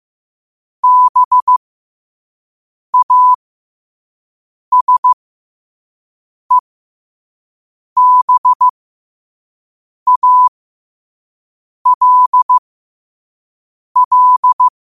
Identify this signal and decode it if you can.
Morse code, heard as a keyed tone, about 15 wpm, spaced as for 5 wpm, 1000 Hz. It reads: BASEBALL
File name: qotdfiles / 5wpm